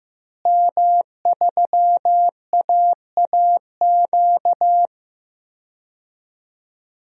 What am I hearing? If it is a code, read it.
Morse code, heard as a keyed tone, about 15 wpm, 700 Hz. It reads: M3AAQ
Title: DXCC-100 Call Signs - 5 Characters - Repeated 3x - 15wpm